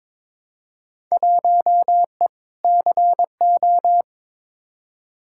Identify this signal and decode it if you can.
Morse code, heard as a keyed tone, about 22 wpm, 700 Hz. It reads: 1E CO
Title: Field Day Exchanges - 22wpm